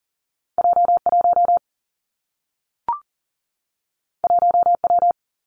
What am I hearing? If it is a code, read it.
Morse code, heard as a keyed tone, about 42 wpm, 700 Hz. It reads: J 1 1 W
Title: Two Challenging Letter-Number Combinations 40wpm